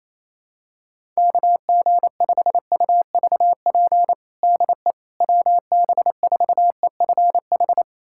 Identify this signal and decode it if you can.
Morse code, heard as a keyed tone, about 27 wpm, 700 Hz. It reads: KG5UVP DE WB4EFH
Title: USA Callsign DE Callsign 28wpm